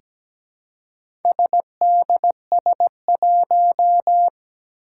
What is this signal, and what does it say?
Morse code, heard as a keyed tone, about 17 wpm, 700 Hz. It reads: SDS1
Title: Four Challenging Letter-Number Combinations 17wpm